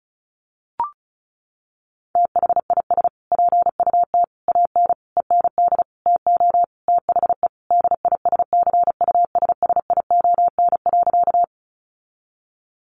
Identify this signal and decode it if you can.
Morse code, heard as a keyed tone, about 34 wpm, 700 Hz. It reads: THIS PUT AN END TO THE DISCUSSION.
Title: Sentences from Top 1100 Words 35wpm